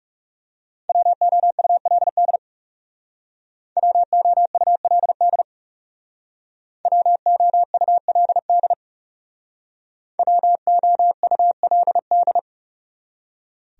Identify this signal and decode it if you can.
Morse code, heard as a keyed tone, about 38 wpm, 700 Hz. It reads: WOULD WOULD WOULD WOULD